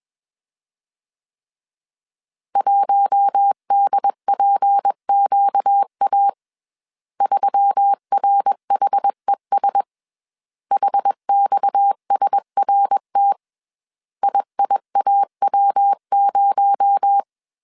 Morse code, heard as a keyed tone, about 21 wpm, 800 Hz. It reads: 1DPQA 3RHES HXSRT IIAW0
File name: CW / DET / 1 6 1500C S A 20wpm 2048